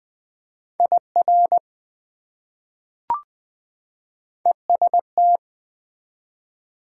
Morse code, heard as a keyed tone, about 21 wpm, 700 Hz. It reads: IR EST
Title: Prefixes and Suffixes 20wpm